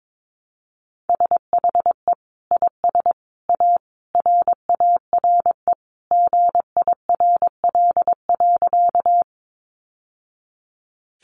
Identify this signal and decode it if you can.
Morse code, heard as a keyed tone, about 22 wpm, 700 Hz. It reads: SHE IS A RARE GIRL.
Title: Sentences from Top 1100 Words 22wpm